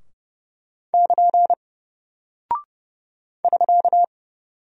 Morse code, heard as a keyed tone, about 30 wpm, 700 Hz. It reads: <KN> <SK>
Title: Single Character (Letter:Number:Punctuation:Prosign) Alt 30wpm